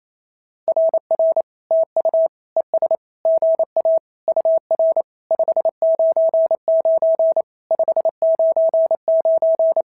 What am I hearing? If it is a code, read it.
Morse code, heard as a keyed tone, about 27 wpm, 650 Hz. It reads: RR TU ES GA UR 599 599
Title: SOTA QSO Practice 28wpm